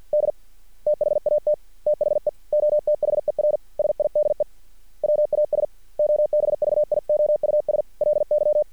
Morse code, heard as a keyed tone, about 50 wpm, 600 Hz. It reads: D THAT THE OTHER SIDE WAS OBVIOUSLY
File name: archived / CRSnet